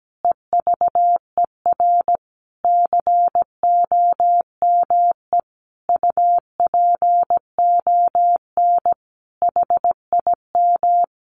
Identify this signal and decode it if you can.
Morse code, heard as a keyed tone, about 17 wpm, 700 Hz. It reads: EVER COME UPON HIM
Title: Sentences from Top 1100 Words 17wpm